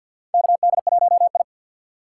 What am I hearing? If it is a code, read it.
Morse code, heard as a keyed tone, about 51 wpm, 700 Hz. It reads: KD1I
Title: USA Call Signs - 3x Character Spacing - 50wpm